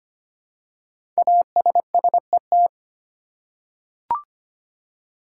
Morse code, heard as a keyed tone, about 25 wpm, 700 Hz. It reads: ASSET
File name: 5 Letter Words - First 3 Letters Given - 25wpm